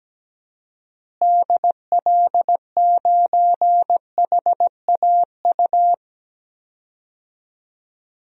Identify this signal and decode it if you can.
Morse code, heard as a keyed tone, about 17 wpm, 700 Hz. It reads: DL9HAU
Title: DXCC-100 Call Signs - 6 Characters - Repeated 3x - 17wpm